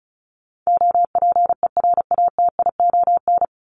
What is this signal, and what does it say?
Morse code, heard as a keyed tone, about 34 wpm, 700 Hz. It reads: OPERATION
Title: Top 1501-1750 Words 35wpm